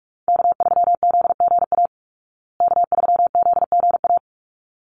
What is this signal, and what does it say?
Morse code, heard as a keyed tone, about 44 wpm, 700 Hz. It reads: K3ZGA K3ZGA
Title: Call Signs Repeated 3x - 45wpm